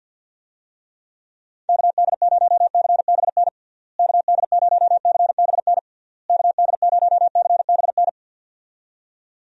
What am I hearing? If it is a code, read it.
Morse code, heard as a keyed tone, about 52 wpm, 700 Hz. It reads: KD0CBN KD0CBN KD0CBN
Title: USA Call Signs - 6 Characters - Repeated 3x - 50wpm